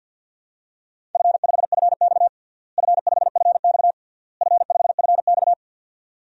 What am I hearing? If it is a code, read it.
Morse code, heard as a keyed tone, about 53 wpm, 700 Hz. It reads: F5FX F5FX F5FX
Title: DXCC-100 Call Signs - 4 Characters - Repeated 3x - 50wpm